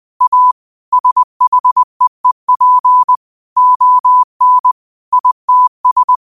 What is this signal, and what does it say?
Morse code, heard as a keyed tone, about 20 wpm, 1000 Hz. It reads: A SHEEP ON ITS
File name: qotdfiles / 20wpm